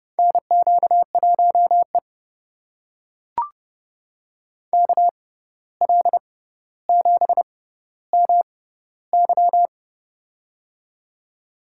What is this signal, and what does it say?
Morse code, heard as a keyed tone, about 30 wpm, spaced as for 14 wpm, 700 Hz. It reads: NQ1E KL7MY
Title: USA Call Signs - 6x Character Spacing - 30wpm